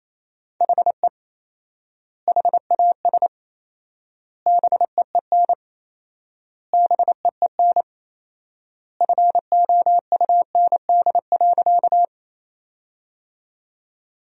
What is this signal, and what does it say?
Morse code, heard as a keyed tone, about 28 wpm, 700 Hz. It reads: HE HAS BEEN BEEN FOUND.
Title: 4x Word-Spacing - Sentences from Top 300 Words - 28wpm